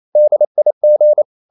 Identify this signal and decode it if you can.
Morse code, heard as a keyed tone, about 28 wpm, 600 Hz. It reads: DIG
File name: Ferie QRQ 28wpm